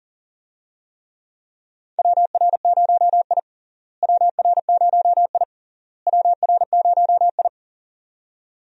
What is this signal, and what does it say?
Morse code, heard as a keyed tone, about 40 wpm, 700 Hz. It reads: WR0I WR0I WR0I
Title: USA Call Signs - 4 Characters - Repeated 3x - 40wpm